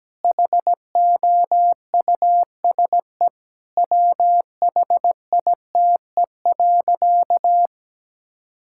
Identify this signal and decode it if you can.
Morse code, heard as a keyed tone, about 17 wpm, 700 Hz. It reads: HOUSE WHITE.
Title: Sentences from Top 1000 Words - Review Part 6 - 17wpm